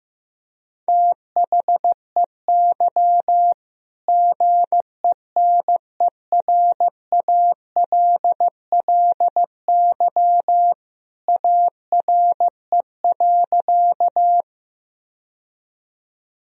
Morse code, heard as a keyed tone, about 15 wpm, 700 Hz. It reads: THEY GENERALLY ARE.
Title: Sentences from Top 1000 Words - Review Part 4 - 15wpm